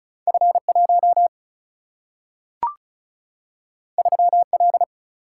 Morse code, heard as a keyed tone, about 36 wpm, 700 Hz. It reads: F 1 3 L
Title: Single-Letter Single-Number 35wpm